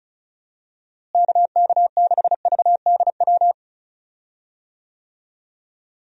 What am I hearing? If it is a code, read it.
Morse code, heard as a keyed tone, about 36 wpm, 700 Hz. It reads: KK6VDW
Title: USA Call Signs - 6 Characters - Repeated 3x - 35wpm